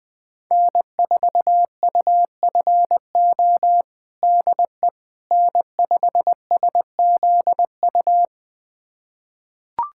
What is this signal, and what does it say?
Morse code, heard as a keyed tone, about 20 wpm, 700 Hz. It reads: N4UFO DE N5SZU
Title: USA Callsign DE Callsign - 5 Characters - 20wpm